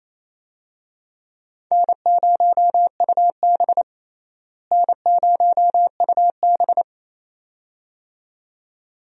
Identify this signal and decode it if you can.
Morse code, heard as a keyed tone, about 27 wpm, 700 Hz. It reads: N0UB N0UB
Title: USA Call Signs - 4 Characters - Repeated 2x - 28wpm